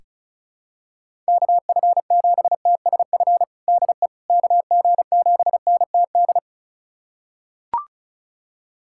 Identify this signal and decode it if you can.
Morse code, heard as a keyed tone, about 34 wpm, 690 Hz. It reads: KF7TSF DE KG7NTD
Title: USA Callsign DE Callsign - 6 Characters - 35wpm